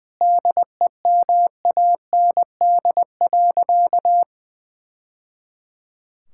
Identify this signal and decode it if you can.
Morse code, heard as a keyed tone, about 21 wpm, 700 Hz. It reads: DEMAND.
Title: Sentences from Top 1000 Words - Review Part 5 - 20wpm